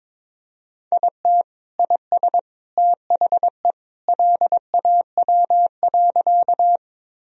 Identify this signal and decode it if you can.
Morse code, heard as a keyed tone, about 22 wpm, 700 Hz. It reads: IT IS THE LAW.
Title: Sentences from Top 1000 Words - Review Part 3 - 22wpm